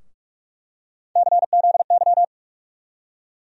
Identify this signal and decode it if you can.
Morse code, heard as a keyed tone, about 41 wpm, 700 Hz. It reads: CZY
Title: USA Call Sign Postfixes - 3 Letters 45wpm